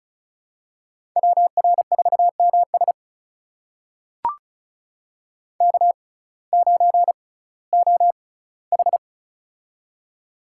Morse code, heard as a keyed tone, about 36 wpm, spaced as for 19 wpm, 700 Hz. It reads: WR4MS K9OH
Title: USA Call Signs - 6x Character Spacing - 35wpm